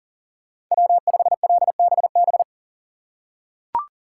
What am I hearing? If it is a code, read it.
Morse code, heard as a keyed tone, about 38 wpm, 700 Hz. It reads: W5LBB
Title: USA Call Signs - 5 Characters - 4x Character Spacing - 40wpm